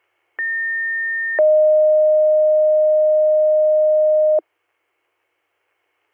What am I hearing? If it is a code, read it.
Two-tone page, heard as a one-second tone, then a three-second tone, about 1800 then 620 Hz. It is heard from 0.4 to 4.4 s.